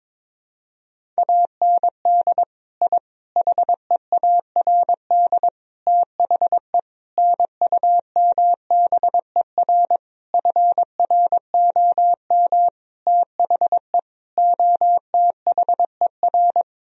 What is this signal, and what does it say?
Morse code, heard as a keyed tone, about 22 wpm, 700 Hz. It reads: AND I HEARD THE NUMBER FROM THE OTHER
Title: Sentences from Top 1000 Words - Review Part 1 - 22wpm